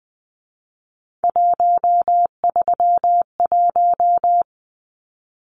Morse code, heard as a keyed tone, about 20 wpm, 700 Hz. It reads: 131
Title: Numbers - 3-Digit 20wpm